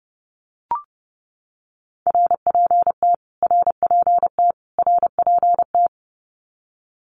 Morse code, heard as a keyed tone, about 30 wpm, 700 Hz. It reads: RPT RPT RPT
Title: Abbreviations Repeated 3x - 30wpm